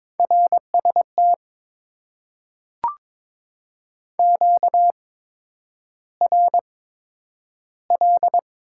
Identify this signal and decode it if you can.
Morse code, heard as a keyed tone, about 22 wpm, spaced as for 9 wpm, 700 Hz. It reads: RST QRL
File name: Top 90 QSO Elements - 8x Character Spacing - 22wpm